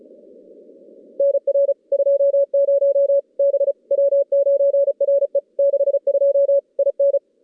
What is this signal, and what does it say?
Morse code, heard as a keyed tone, about 34 wpm, 560 Hz. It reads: NR 20 B W9RE 62 IN
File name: contests / CW-SS-k5tr- -2100